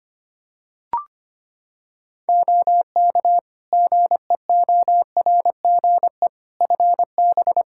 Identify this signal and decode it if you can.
Morse code, heard as a keyed tone, about 25 wpm, 700 Hz. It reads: OK GEORGE FB
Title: K1USN SST QSO Practice 25wpm